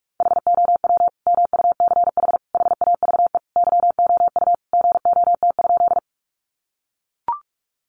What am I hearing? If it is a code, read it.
Morse code, heard as a keyed tone, about 44 wpm, 700 Hz. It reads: HOW MUCH HAVE YOU GOT?
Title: Sentences from Top 300 Words 45wpm